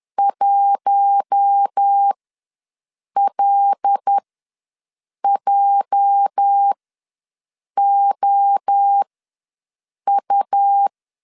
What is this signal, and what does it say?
Morse code, heard as a keyed tone, about 10 wpm, spaced as for 6 wpm, 800 Hz. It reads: 1LJOU
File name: LEZIONI / 8 YJ45 / 8 YJ4 ws 10wpm 4020